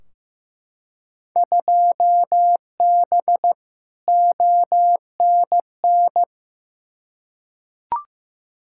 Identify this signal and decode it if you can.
Morse code, heard as a keyed tone, about 15 wpm, 700 Hz. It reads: 2B ONN
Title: Field Day Exchanges Encore - 15wpm